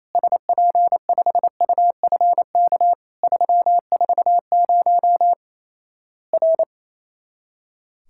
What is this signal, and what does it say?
Morse code, heard as a keyed tone, about 28 wpm, 700 Hz. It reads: SP5UFK 340 R
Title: Worked All Europe DX Contest - QTC Exchange - 28wpm